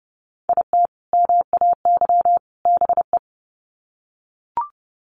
Morse code, heard as a keyed tone, about 30 wpm, 700 Hz. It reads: IT MAY BE
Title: Sets of 3 Words 30wpm